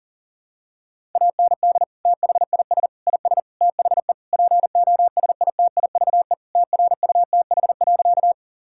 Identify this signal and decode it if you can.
Morse code, heard as a keyed tone, about 40 wpm, 700 Hz. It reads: AND THIS IS THE POSITIVE TRUTH.
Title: Sentences from Top 1000 Words - Review Part 5 - 40wpm